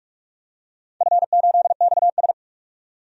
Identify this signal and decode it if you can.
Morse code, heard as a keyed tone, about 44 wpm, 700 Hz. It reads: F8XS
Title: DXCC-100 Call Signs - 4 Characters - 45wpm